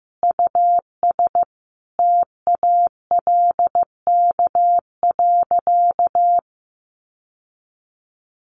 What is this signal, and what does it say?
Morse code, heard as a keyed tone, about 15 wpm, 700 Hz. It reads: US TALK.
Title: Sentences from Top 1300 Words 15wpm